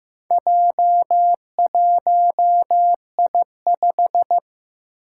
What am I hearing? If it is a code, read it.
Morse code, heard as a keyed tone, about 15 wpm, 700 Hz. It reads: J1I5
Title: Four Challenging Letter-Number Combinations 15wpm